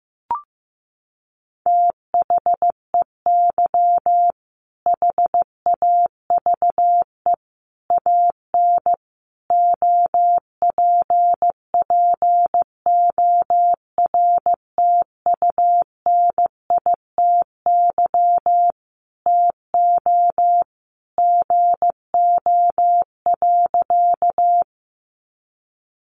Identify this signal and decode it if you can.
Morse code, heard as a keyed tone, about 14 wpm, 700 Hz. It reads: THEY HAVE AN OPPORTUNITY TO GO.
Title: Sentences from Top 1000 Words 15wpm